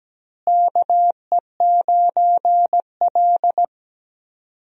Morse code, heard as a keyed tone, about 17 wpm, 700 Hz. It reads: KE9L
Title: USA Call Signs - 4 Characters - Repeated 3x - 17wpm